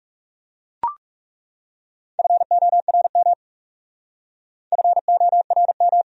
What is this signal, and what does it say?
Morse code, heard as a keyed tone, about 36 wpm, 700 Hz. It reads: FORM FORM
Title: Speed-Racing Top 200 Words 30wpm